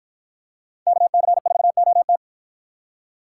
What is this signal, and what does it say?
Morse code, heard as a keyed tone, about 53 wpm, 700 Hz. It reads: KC4OT